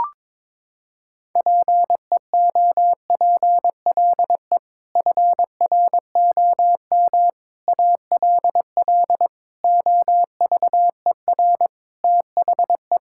Morse code, heard as a keyed tone, about 22 wpm, 700 Hz. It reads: PEOPLE FROM ALL OVER THE
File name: practice / 22wpm / Sets of 6 Words - 22wpm